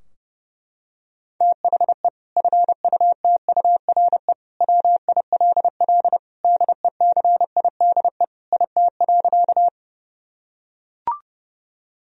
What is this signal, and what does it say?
Morse code, heard as a keyed tone, about 32 wpm, 700 Hz. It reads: THE FUTURE WILL DECIDE IT.
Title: Sentences from Top 1000 Words - Review Part 3 - 30wpm